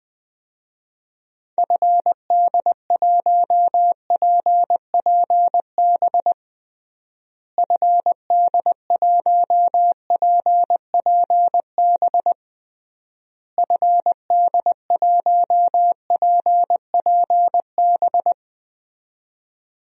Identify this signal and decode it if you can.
Morse code, heard as a keyed tone, about 21 wpm, 700 Hz. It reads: FD1PPB FD1PPB FD1PPB
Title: DXCC-100 Call Signs - 6 Characters - Repeated 3x - 20wpm